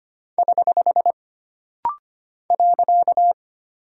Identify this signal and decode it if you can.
Morse code, heard as a keyed tone, about 25 wpm, 700 Hz. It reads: <HH> .
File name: Single Punctuation-Prosign - Rapid-Fire - 25wpm